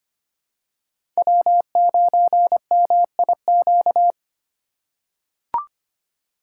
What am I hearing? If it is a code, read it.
Morse code, heard as a keyed tone, about 25 wpm, 700 Hz. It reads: W9MIQ
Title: USA Call Signs - 5 Characters - 3x Character Spacing - 25wpm